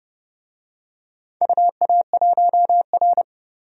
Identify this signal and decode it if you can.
Morse code, heard as a keyed tone, about 31 wpm, 700 Hz. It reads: UA1R
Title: DXCC-100 Call Signs - 4 Characters - 30wpm